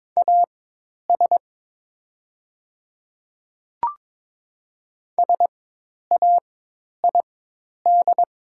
Morse code, heard as a keyed tone, about 22 wpm, spaced as for 10 wpm, 690 Hz. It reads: AS SAID
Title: 4x Character Spacing - Top 300 Words - 22wpm